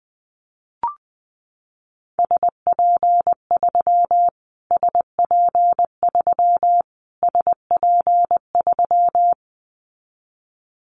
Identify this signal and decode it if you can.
Morse code, heard as a keyed tone, about 20 wpm, 700 Hz. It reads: SP3 SP3 SP3
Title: Callsign Prefixes for Top 25 DXCC - Repeated 3x - 20wpm